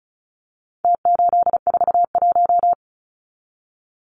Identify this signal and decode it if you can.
Morse code, heard as a keyed tone, about 34 wpm, 700 Hz. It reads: T841
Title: Numbers - Contest Style - 4-Digit 35wpm